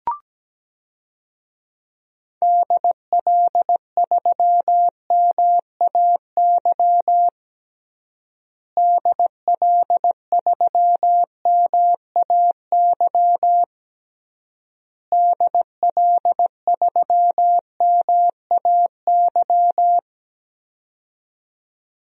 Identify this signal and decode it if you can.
Morse code, heard as a keyed tone, about 17 wpm, 700 Hz. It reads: DL3MAY DL3MAY DL3MAY
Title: DXCC-100 Call Signs - 6 Characters - Repeated 3x - 17wpm